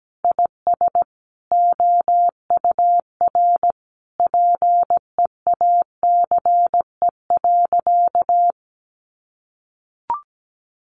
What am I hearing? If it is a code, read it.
Morse code, heard as a keyed tone, about 17 wpm, 700 Hz. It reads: IS OUR PEACE.
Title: Sentences from Top 1500 Words 17wpm